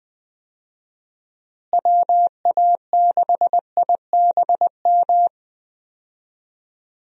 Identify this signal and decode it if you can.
Morse code, heard as a keyed tone, about 20 wpm, 700 Hz. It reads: WA6IBM
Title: USA Call Signs - 6 Characters - Repeated 2x - 20wpm